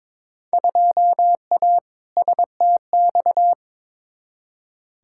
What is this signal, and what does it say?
Morse code, heard as a keyed tone, about 22 wpm, 700 Hz. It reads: 2A STX
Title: Field Day Exchanges - 22wpm